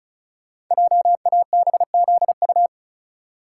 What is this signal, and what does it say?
Morse code, heard as a keyed tone, about 36 wpm, 700 Hz. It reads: JABZU
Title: Code Groups - 5 Letters - 35wpm